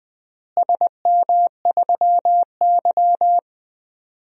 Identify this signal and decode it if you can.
Morse code, heard as a keyed tone, about 20 wpm, 700 Hz. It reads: SM3Y
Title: DXCC-100 Call Signs - 4 Characters - Repeated 2x - 20wpm